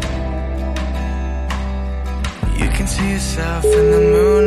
Ringback tone (Europe, 425 Hz), with noise.